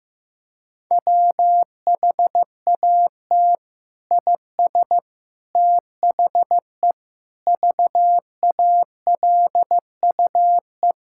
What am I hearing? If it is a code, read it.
Morse code, heard as a keyed tone, about 15 wpm, 700 Hz. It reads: WHAT IS THE VALUE
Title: Sentences from Top 1000 Words - Review Part 6 - 15wpm